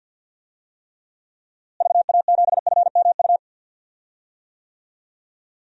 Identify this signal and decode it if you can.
Morse code, heard as a keyed tone, about 53 wpm, 700 Hz. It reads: VA7FMU
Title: DXCC-100 Call Signs - 6 Characters - Repeated 2x - 50wpm